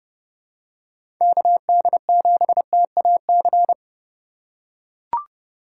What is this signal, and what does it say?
Morse code, heard as a keyed tone, about 31 wpm, 700 Hz. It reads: KD7TAC
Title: USA Call Signs - 6 Characters - 8x Character Spacing - 30wpm